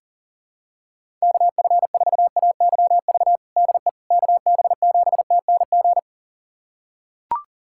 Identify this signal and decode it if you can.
Morse code, heard as a keyed tone, about 37 wpm, 700 Hz. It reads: KF4AYV DE KB7TNG
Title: USA Callsign DE Callsign - 6 Characters - 40wpm